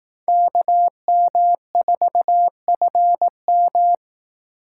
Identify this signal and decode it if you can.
Morse code, heard as a keyed tone, about 18 wpm, 700 Hz. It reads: KM4FM
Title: Speed Racing - USA Call Signs 15wpm